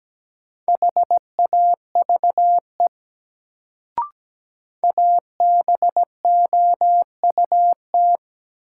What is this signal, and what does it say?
Morse code, heard as a keyed tone, about 17 wpm, 700 Hz. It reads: HAVE ABOUT
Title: Sentences from Top 100 Words - New Word Drill - 17wpm